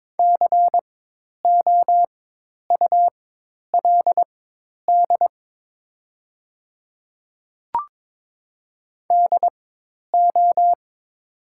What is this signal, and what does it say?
Morse code, heard as a keyed tone, about 22 wpm, spaced as for 10 wpm, 700 Hz. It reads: COULD DO
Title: 4x Character Spacing - Top 300 Words - 22wpm